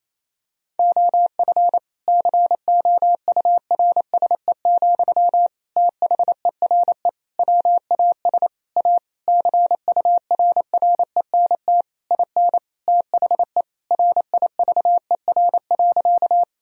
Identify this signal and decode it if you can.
Morse code, heard as a keyed tone, about 28 wpm, 700 Hz. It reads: OF COURSE, THERE WAS A CURRENT IN THE RIVER.
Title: Sentences from Top 1000 Words - Review Part 4 - 28wpm